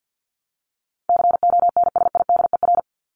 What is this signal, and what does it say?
Morse code, heard as a keyed tone, about 50 wpm, 700 Hz. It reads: CONSIDER
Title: ING Suffix - Top 100 Words with Base Words 50wpm